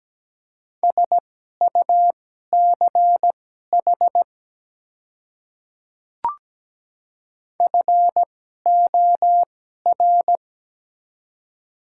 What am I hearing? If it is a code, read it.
Morse code, heard as a keyed tone, about 17 wpm, spaced as for 12 wpm, 700 Hz. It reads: SUCH FOR
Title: 2x Character Spacing - Top 200 Words - 17wpm